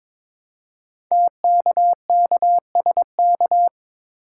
Morse code, heard as a keyed tone, about 22 wpm, 700 Hz. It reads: TKKSK
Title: Code Groups - 5 Letters - 22wpm